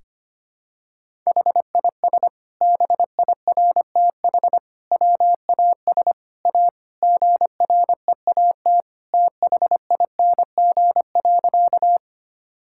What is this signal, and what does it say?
Morse code, heard as a keyed tone, about 25 wpm, 700 Hz. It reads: HIS BIRTH WAS A GREAT THING.
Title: Sentences from Top 1000 Words - Review Part 5 - 25wpm